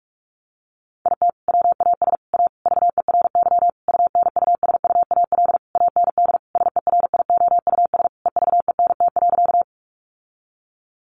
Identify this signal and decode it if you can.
Morse code, heard as a keyed tone, about 44 wpm, 700 Hz. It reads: IT WAS A VERY UNUSUAL AND SERIOUS EVENT.
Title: Sentences from Top 1400 Words 45wpm